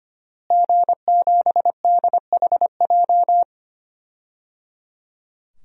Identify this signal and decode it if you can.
Morse code, heard as a keyed tone, about 25 wpm, 700 Hz. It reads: G7DHJ
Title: DXCC-100 Call Signs - 5 Characters - Repeated 3x - 25wpm